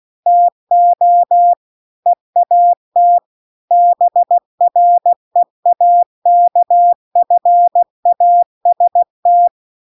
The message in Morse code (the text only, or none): TO EAT BREAKFAST